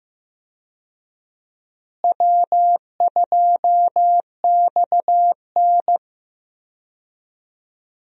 Morse code, heard as a keyed tone, about 15 wpm, 700 Hz. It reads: W2XN
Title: USA Call Signs - 4 Characters - Repeated 2x - 15wpm